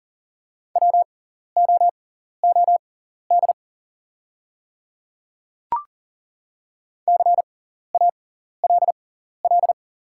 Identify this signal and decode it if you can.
Morse code, heard as a keyed tone, about 40 wpm, spaced as for 14 wpm, 690 Hz. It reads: WOOD CALL
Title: 6x Character Spacing - Top 300 Words - 40wpm